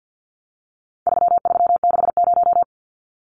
Morse code, heard as a keyed tone, about 50 wpm, 700 Hz. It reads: 3360
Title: Numbers - 4-Digit 50wpm